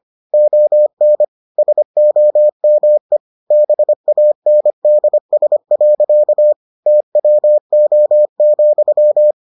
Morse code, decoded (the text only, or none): ON SOME BANDS. TWO,